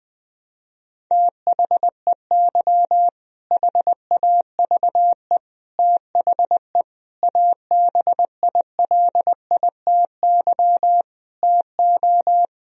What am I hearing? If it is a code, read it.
Morse code, heard as a keyed tone, about 20 wpm, 700 Hz. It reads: THEY HAVE THE ABILITY TO
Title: Sentences from Top 1000 Words - Review Part 3 - 20wpm